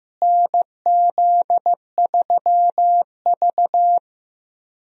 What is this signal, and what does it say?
Morse code, heard as a keyed tone, about 15 wpm, 700 Hz. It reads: NZ3V
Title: USA Call Signs - 4 Characters - Repeated 3x - 15wpm